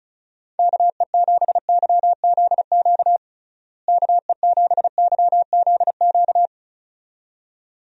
Morse code, heard as a keyed tone, about 34 wpm, 700 Hz. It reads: KE7YZQ KE7YZQ
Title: USA Call Signs - 6 Characters - Repeated 2x - 35wpm